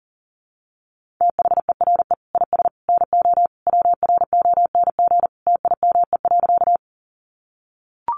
Morse code, heard as a keyed tone, about 40 wpm, 700 Hz. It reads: THERE IS NO WRONG TIME.
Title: Sentences from Top 900 Words 40wpm